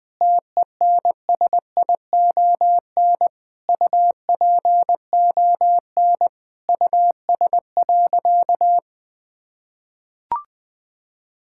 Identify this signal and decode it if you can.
Morse code, heard as a keyed tone, about 20 wpm, 700 Hz. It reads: TENSION UPON US.
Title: Sentences from Top 1600 Words 20wpm